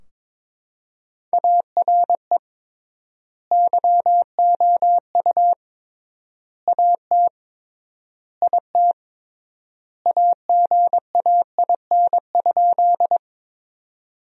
Morse code, heard as a keyed tone, about 22 wpm, 700 Hz. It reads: ARE YOU AT IT AGAIN?
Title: 3x Word-Spacing - Sentences from Top 200 Words - 22wpm